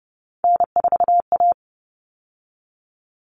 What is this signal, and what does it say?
Morse code, heard as a keyed tone, about 30 wpm, 700 Hz. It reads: N4A
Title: All Cut Numbers - 3-Digit 30wpm